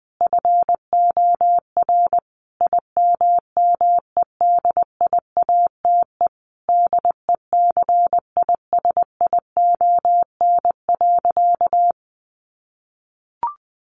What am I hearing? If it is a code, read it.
Morse code, heard as a keyed tone, about 20 wpm, 700 Hz. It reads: FOR IMMEDIATE DECISION.
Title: Sentences from Top 1200 Words 20wpm